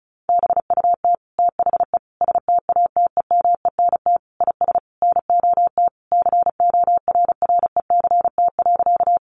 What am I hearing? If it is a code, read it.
Morse code, heard as a keyed tone, about 34 wpm, 700 Hz. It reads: BUT THE STATEMENT IS NOT CORRECT.
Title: Sentences from Top 1200 Words 35wpm